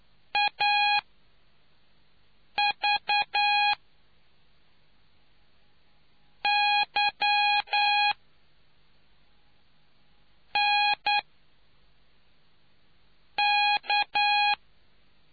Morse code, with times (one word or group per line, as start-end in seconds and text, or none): 0.4-3.7 AV
6.4-8.1 Y
10.6-11.2 N
13.4-14.5 K